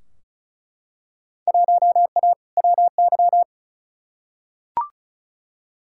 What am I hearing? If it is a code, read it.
Morse code, heard as a keyed tone, about 36 wpm, 700 Hz. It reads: 1A WY
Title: Field Day Exchanges Encore - 35wpm